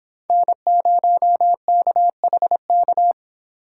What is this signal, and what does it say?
Morse code, heard as a keyed tone, about 25 wpm, 700 Hz. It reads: N0KHK